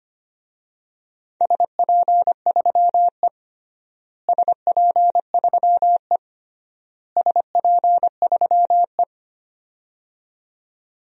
Morse code, heard as a keyed tone, about 25 wpm, 700 Hz. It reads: SP3E SP3E SP3E
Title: DXCC-100 Call Signs - 4 Characters - Repeated 3x - 25wpm